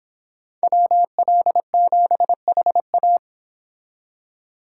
Morse code, heard as a keyed tone, about 26 wpm, 700 Hz. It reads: WL7HA